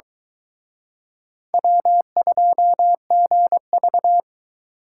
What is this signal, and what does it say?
Morse code, heard as a keyed tone, about 24 wpm, 700 Hz. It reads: W2GV